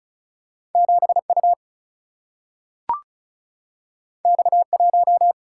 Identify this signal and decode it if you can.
Morse code, heard as a keyed tone, about 36 wpm, 700 Hz. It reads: 7 U X 1
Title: Single-Letter Single-Number 35wpm